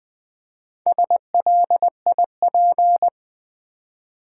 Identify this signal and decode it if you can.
Morse code, heard as a keyed tone, about 20 wpm, 700 Hz. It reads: SLIP
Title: Four Letter Words - 20wpm